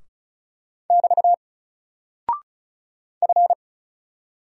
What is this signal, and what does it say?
Morse code, heard as a keyed tone, about 36 wpm, 700 Hz. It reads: <BT> F
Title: Single Character (Letter:Number:Punctuation:Prosign) 35wpm